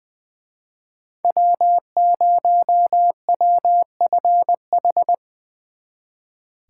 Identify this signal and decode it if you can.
Morse code, heard as a keyed tone, about 23 wpm, 700 Hz. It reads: W0WFH
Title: Speed Racing - USA Call Signs 20wpm